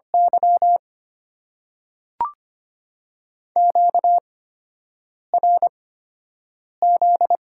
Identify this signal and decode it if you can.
Morse code, heard as a keyed tone, about 25 wpm, spaced as for 7 wpm, 690 Hz. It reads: Y QRZ
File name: Top 90 QSO Elements - 8x Character Spacing - 25wpm